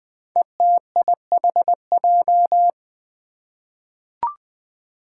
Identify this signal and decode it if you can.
Morse code, heard as a keyed tone, about 20 wpm, 700 Hz. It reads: ETIHJ
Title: Five Challenging Letter-Number Combinations 20wpm